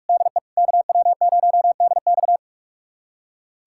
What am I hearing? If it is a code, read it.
Morse code, heard as a keyed tone, about 41 wpm, 700 Hz. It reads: DE KW0DX
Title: USA Callsign DE Callsign - 5 Characters - 45wpm